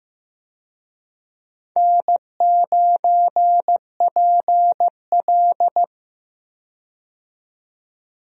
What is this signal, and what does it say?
Morse code, heard as a keyed tone, about 15 wpm, 700 Hz. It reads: N9PL
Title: USA Call Signs - 4 Characters - 15wpm